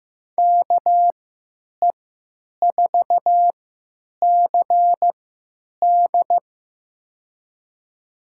Morse code, heard as a keyed tone, about 16 wpm, spaced as for 9 wpm, 700 Hz. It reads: KE4CD
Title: USA Call Signs - 5 Characters - 3x Character Spacing - 15wpm